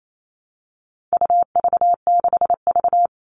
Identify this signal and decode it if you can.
Morse code, heard as a keyed tone, about 28 wpm, 700 Hz. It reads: UV6V
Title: All Cut Numbers - 4-Digit 28wpm